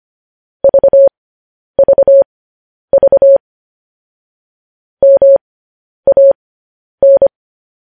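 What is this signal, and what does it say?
Morse code, heard as a keyed tone, about 25 wpm, spaced as for 10 wpm, 550 Hz. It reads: VVV MAN
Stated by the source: Morse news headlines sent at 10 words per minute.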